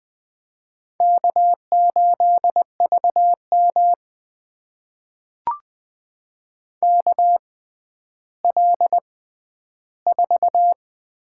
Morse code, heard as a keyed tone, about 20 wpm, spaced as for 11 wpm, 700 Hz. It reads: K8VM KL4
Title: USA Call Signs - 4 Characters - 6x Character Spacing - 20wpm